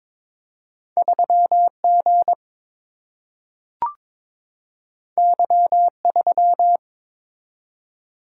Morse code, heard as a keyed tone, about 22 wpm, 700 Hz. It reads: 3 G Y 3
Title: Single-Letter Single-Number 22wpm